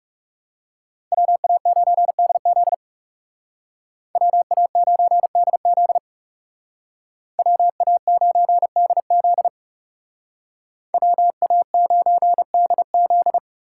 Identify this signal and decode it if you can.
Morse code, heard as a keyed tone, about 37 wpm, 700 Hz. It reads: WA9DZ WA9DZ WA9DZ WA9DZ
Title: Speed Racing - USA 2x2 Call Signs 30wpm